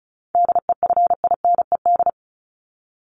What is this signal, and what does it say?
Morse code, heard as a keyed tone, about 34 wpm, 700 Hz. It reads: DEFINED
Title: ED Suffix - Top 100 Words with Base Words 35wpm